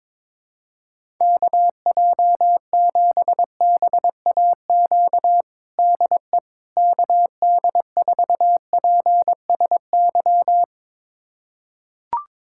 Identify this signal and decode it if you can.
Morse code, heard as a keyed tone, about 22 wpm, 700 Hz. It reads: KJ7BAQ DE KD4PSY
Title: USA Callsign DE Callsign 22wpm